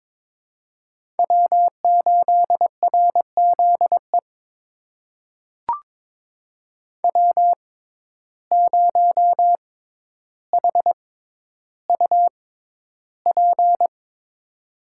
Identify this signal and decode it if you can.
Morse code, heard as a keyed tone, about 22 wpm, spaced as for 11 wpm, 700 Hz. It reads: W8RZE W0HUP
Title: USA Call Signs - 5 Characters - 6x Character Spacing - 22wpm